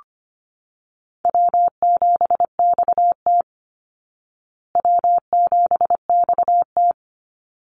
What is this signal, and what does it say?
Morse code, heard as a keyed tone, about 25 wpm, 700 Hz. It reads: W7XT W7XT
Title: Easy Call Signs Repeated 2x - 25wpm